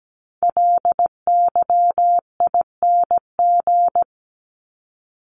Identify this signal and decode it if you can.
Morse code, heard as a keyed tone, about 17 wpm, 700 Hz. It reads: LYING
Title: ING Suffix - Top 100 Words with Base Words 17wpm